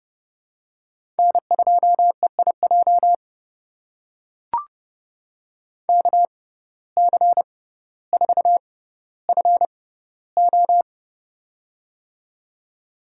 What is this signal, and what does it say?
Morse code, heard as a keyed tone, about 32 wpm, spaced as for 16 wpm, 700 Hz. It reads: N2EIJ KC4FO
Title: USA Call Signs - 5 Characters - 6x Character Spacing - 30wpm